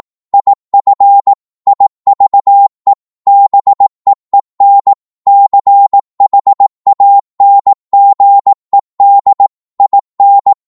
Morse code, heard as a keyed tone, about 18 wpm, 800 Hz. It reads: IF IVE BEEN CHANGED IN
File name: Morse / psg-18wpm